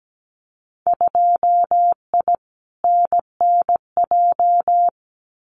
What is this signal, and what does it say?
Morse code, heard as a keyed tone, about 17 wpm, 700 Hz. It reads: 2I NNJ
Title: Winter Field Day Exchanges Encore 17wpm